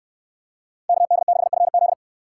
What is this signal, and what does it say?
Morse code, heard as a keyed tone, about 63 wpm, 700 Hz. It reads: KD6FZ